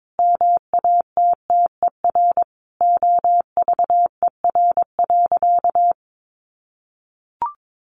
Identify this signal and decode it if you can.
Morse code, heard as a keyed tone, about 22 wpm, 700 Hz. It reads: MATTER OVER.
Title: Sentences from Top 700 Words 22wpm